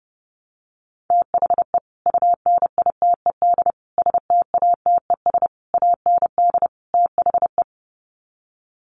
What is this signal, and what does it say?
Morse code, heard as a keyed tone, about 30 wpm, 700 Hz. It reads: THE UNITED STATES AND THE
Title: Sets of 5 Words 30wpm